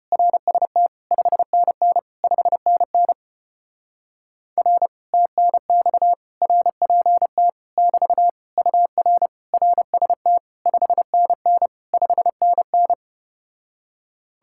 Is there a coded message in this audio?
Yes: Morse code from 0.1 to 12.9 s, heard as a keyed tone, about 33 wpm, 700 Hz. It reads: RST 5NN 5NN R TNX RPT <BT> UR RST 5NN 5NN